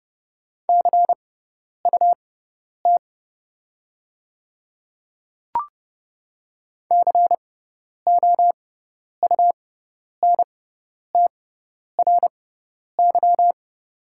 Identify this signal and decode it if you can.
Morse code, heard as a keyed tone, about 30 wpm, spaced as for 10 wpm, 700 Hz. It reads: CUT COUNTRY
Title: 6x Character Spacing - Top 300 Words - 30wpm